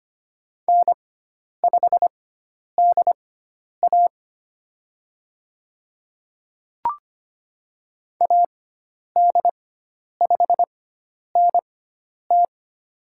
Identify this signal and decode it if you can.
Morse code, heard as a keyed tone, about 25 wpm, spaced as for 10 wpm, 700 Hz. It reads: N5DA AD5NT
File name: Lesson 16g - Letter c - Call Signs 25wpm